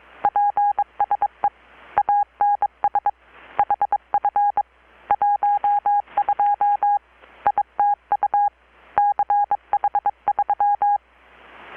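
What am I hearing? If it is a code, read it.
Morse code, heard as a keyed tone, about 22 wpm, 820 Hz. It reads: PSE ANS HF 12 ITU CH3
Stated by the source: Morse from KPH's Night of Nights XX maritime radiotelegraphy event, received in Chiapas, Mexico about 3,800 km away on a Kenwood R-600 with a grounded 10 m long-wire antenna.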